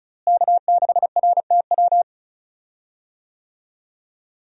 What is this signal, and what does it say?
Morse code, heard as a keyed tone, about 34 wpm, 700 Hz. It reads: K6RTW
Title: USA Call Signs - 5 Characters - Repeated 2x - 35wpm